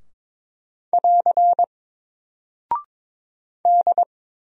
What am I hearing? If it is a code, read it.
Morse code, heard as a keyed tone, about 22 wpm, 700 Hz. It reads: <AR> D
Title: Single Character (Letter:Number:Punctuation:Prosign) 22wpm